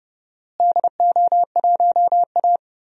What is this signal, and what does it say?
Morse code, heard as a keyed tone, about 30 wpm, 700 Hz. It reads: DO1A
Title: DXCC-100 Call Signs - 4 Characters - Repeated 3x - 30wpm